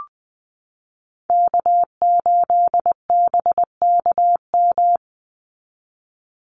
Morse code, heard as a keyed tone, about 19 wpm, 700 Hz. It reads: K8BKM
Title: Call Signs Repeated 3x - 20wpm